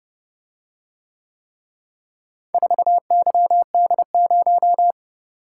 Morse code, heard as a keyed tone, about 29 wpm, 700 Hz. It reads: 4YD0